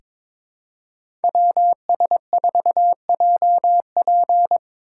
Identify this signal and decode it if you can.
Morse code, heard as a keyed tone, about 22 wpm, 700 Hz. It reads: WS4JP
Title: USA Call Signs - 5 Characters - 4x Character Spacing - 22wpm